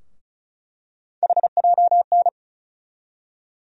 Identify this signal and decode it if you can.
Morse code, heard as a keyed tone, about 36 wpm, 700 Hz. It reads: HJN
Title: USA Call Sign Postfixes - 3 Letters 35wpm